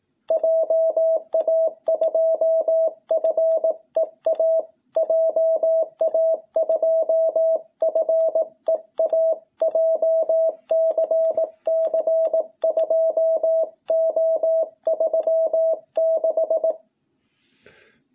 Morse code, heard as a keyed tone, about 18 wpm, 660 Hz. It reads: JA2FEA JA2FEA JCC 2O36
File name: jh8cbh / 2021cw